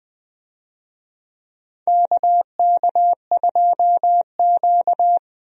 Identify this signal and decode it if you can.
Morse code, heard as a keyed tone, about 20 wpm, 700 Hz. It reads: KK2Q
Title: USA Call Signs - 4 Characters - Repeated 3x - 20wpm